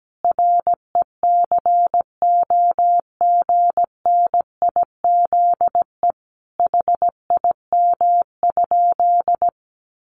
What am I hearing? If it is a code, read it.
Morse code, heard as a keyed tone, about 17 wpm, 700 Hz. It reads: RECOGNIZE HIM?
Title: Sentences from Top 1400 Words 17wpm